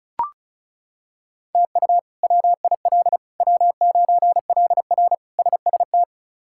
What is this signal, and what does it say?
Morse code, heard as a keyed tone, about 36 wpm, 700 Hz. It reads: TU WIL W9LR SST
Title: K1USN SST QSO Practice - 35wpm